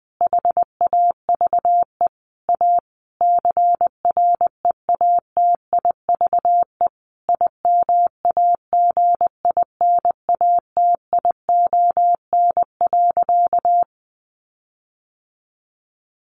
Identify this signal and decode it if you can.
Morse code, heard as a keyed tone, about 20 wpm, 700 Hz. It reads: HAVE A CREATIVE IMAGINATION.
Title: Sentences from Top 1300 Words 20wpm